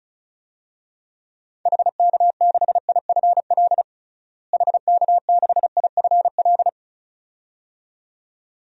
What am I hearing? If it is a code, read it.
Morse code, heard as a keyed tone, about 36 wpm, 700 Hz. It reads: HK6IFL HK6IFL
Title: DXCC-100 Call Signs - 6 Characters - Repeated 2x - 35wpm